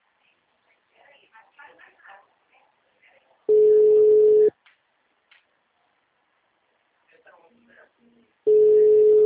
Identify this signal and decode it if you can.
Ringback tone (Europe, 425 Hz), with speech.
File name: out- -2018- - - .36879